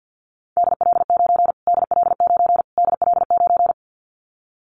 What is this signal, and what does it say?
Morse code, heard as a keyed tone, about 50 wpm, 700 Hz. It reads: DL9 DL9 DL9
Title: Callsign Prefixes for Top 25 DXCC - Repeated 3x - 50wpm